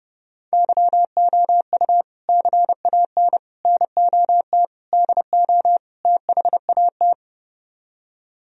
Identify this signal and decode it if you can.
Morse code, heard as a keyed tone, about 32 wpm, 700 Hz. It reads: YOU CAN NOT DO THAT